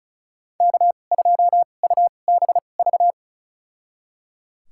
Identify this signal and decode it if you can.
Morse code, heard as a keyed tone, about 34 wpm, spaced as for 25 wpm, 700 Hz. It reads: K2UBV
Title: USA Call Signs - 5 Characters - 2x Character Spacing - 35wpm